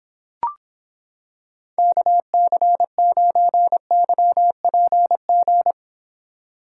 Morse code, heard as a keyed tone, about 26 wpm, 700 Hz. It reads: KC9YPG